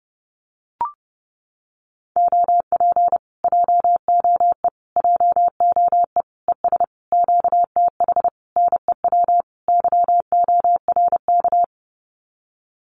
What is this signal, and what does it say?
Morse code, heard as a keyed tone, about 30 wpm, 700 Hz. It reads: OP JOE JOE ES QTH NEW YORK
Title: Common QSO Phrases Encore 30wpm